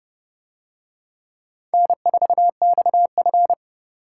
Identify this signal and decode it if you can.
Morse code, heard as a keyed tone, about 29 wpm, 700 Hz. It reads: N4XF